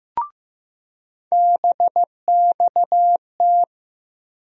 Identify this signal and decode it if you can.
Morse code, heard as a keyed tone, about 15 wpm, 700 Hz. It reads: BXT
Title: USA Call Sign Postfixes - 3 Letters 15wpm